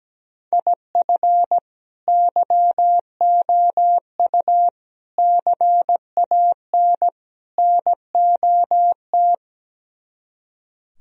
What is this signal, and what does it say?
Morse code, heard as a keyed tone, about 17 wpm, 700 Hz. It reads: IF YOU CAN NOT
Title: Speed-Racing - Sets of 4 Words 17wpm